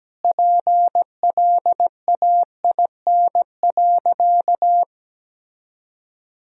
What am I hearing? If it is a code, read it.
Morse code, heard as a keyed tone, about 17 wpm, 700 Hz. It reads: PLAIN.
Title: Sentences from Top 1000 Words - Review Part 2 - 17wpm